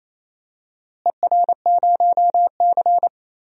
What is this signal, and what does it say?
Morse code, heard as a keyed tone, about 28 wpm, 700 Hz. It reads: ER0C
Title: DXCC-100 Call Signs - 4 Characters - Repeated 3x - 28wpm